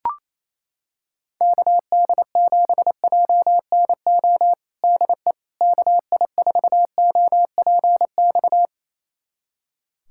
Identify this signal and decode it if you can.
Morse code, heard as a keyed tone, about 28 wpm, 700 Hz. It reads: KD7JNO DE KI4OPX
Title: USA Callsign DE Callsign 28wpm